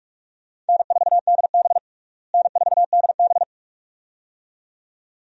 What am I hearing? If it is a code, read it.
Morse code, heard as a keyed tone, about 46 wpm, 700 Hz. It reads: N4DB N4DB
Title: USA Call Signs - 4 Characters - Repeated 2x - 45wpm